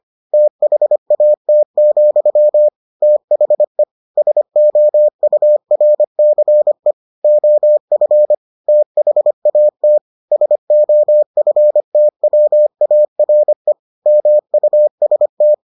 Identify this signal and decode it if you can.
Morse code, heard as a keyed tone, about 25 wpm, 600 Hz. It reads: THAT, THE SOURCE OF THAT SOFTWARE MUST